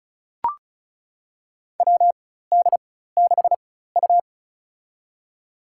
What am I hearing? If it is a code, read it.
Morse code, heard as a keyed tone, about 36 wpm, spaced as for 16 wpm, 700 Hz. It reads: WD6U
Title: USA Call Signs - 4 Characters - 4x Character Spacing - 35wpm